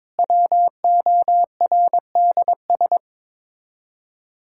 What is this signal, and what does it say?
Morse code, heard as a keyed tone, about 22 wpm, 700 Hz. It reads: WORDS
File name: practice / 22wpm / Sets of 7 Words - 22wpm